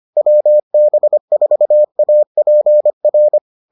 Morse code, heard as a keyed tone, about 25 wpm, 600 Hz. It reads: WB4APR